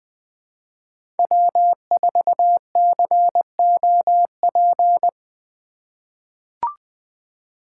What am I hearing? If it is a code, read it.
Morse code, heard as a keyed tone, about 20 wpm, 700 Hz. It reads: W4COP
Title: USA Call Signs - 5 Characters - 4x Character Spacing - 20wpm